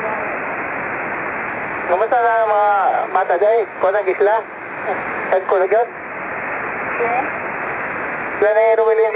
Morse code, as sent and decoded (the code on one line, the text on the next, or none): none